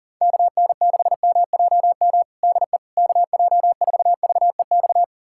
Morse code, heard as a keyed tone, about 40 wpm, 690 Hz. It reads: KN6MJM DE KJ4VEX
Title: USA Callsign DE Callsign - 6 Characters - 40wpm